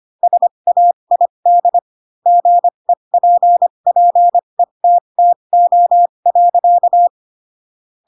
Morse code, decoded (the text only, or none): SAID GEPPETTO.